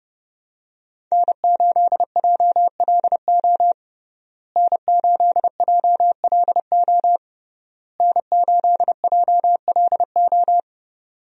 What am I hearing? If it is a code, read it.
Morse code, heard as a keyed tone, about 30 wpm, 700 Hz. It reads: N8JLO N8JLO N8JLO
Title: USA Call Signs - 5 Characters - Repeated 3x - 30wpm